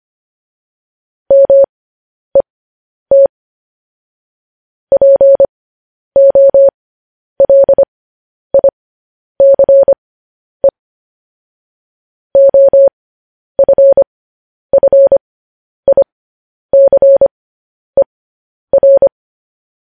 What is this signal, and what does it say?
Morse code, heard as a keyed tone, about 25 wpm, spaced as for 10 wpm, 550 Hz. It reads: MET POLICE OFFICER
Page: news headlines in morse code at 10 wpm